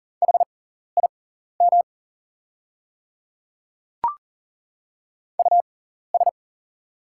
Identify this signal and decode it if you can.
Morse code, heard as a keyed tone, about 40 wpm, spaced as for 14 wpm, 700 Hz. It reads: HIM US
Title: 6x Character Spacing - Top 500 Words - 40wpm